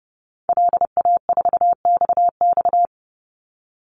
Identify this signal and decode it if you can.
Morse code, heard as a keyed tone, about 30 wpm, 700 Hz. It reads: LA4XX
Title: Call Signs - 30wpm